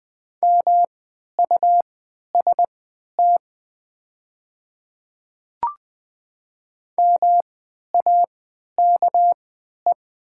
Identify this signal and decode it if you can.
Morse code, heard as a keyed tone, about 20 wpm, spaced as for 11 wpm, 700 Hz. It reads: MUST MAKE
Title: 3x Character Spacing - Top 200 Words - 20wpm